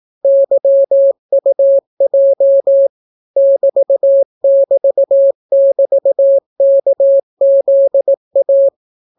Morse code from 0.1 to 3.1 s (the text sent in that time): YUJ